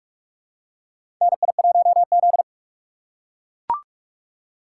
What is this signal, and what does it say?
Morse code, heard as a keyed tone, about 42 wpm, 700 Hz. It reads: NI1Z
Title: USA Call Signs - 4x Character Spacing - 45wpm